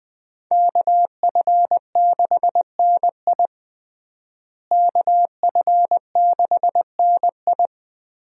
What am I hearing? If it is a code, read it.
Morse code, heard as a keyed tone, about 20 wpm, 700 Hz. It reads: KF6NI KF6NI
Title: USA Call Signs - 5 Characters - Repeated 3x - 20wpm